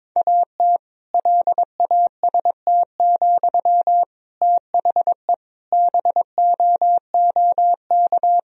Morse code, heard as a keyed tone, about 22 wpm, 700 Hz. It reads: AT LAST, THE BOOK
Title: Sentences from Top 1000 Words - Review Part 3 - 22wpm